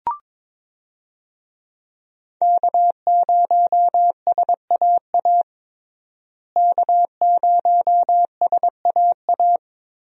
Morse code, heard as a keyed tone, about 22 wpm, 700 Hz. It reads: K0SAA K0SAA
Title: USA Call Signs - 5 Characters - Repeated 3x - 22wpm